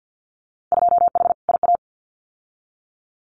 Morse code, heard as a keyed tone, about 50 wpm, 700 Hz. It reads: 2H IA
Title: Winter Field Day Exchanges Encore 50wpm